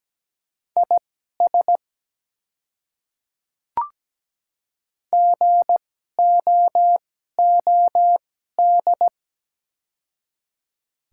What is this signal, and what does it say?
Morse code, heard as a keyed tone, about 17 wpm, spaced as for 12 wpm, 700 Hz. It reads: IS GOOD
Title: 2x Character Spacing - Top 200 Words - 17wpm